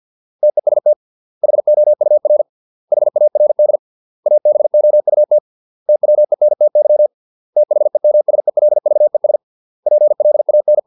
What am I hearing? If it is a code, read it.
Morse code, heard as a keyed tone, about 50 wpm, 600 Hz. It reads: TEST HOUR HARD ABOUT TWENTY THEMSELVES PLAN